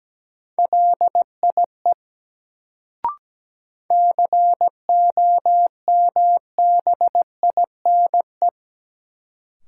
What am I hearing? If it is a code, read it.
Morse code, heard as a keyed tone, about 17 wpm, 700 Hz. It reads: LIE COMBINE
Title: Sentences from Top 1600 Words - New Word Drill Alt - 17wpm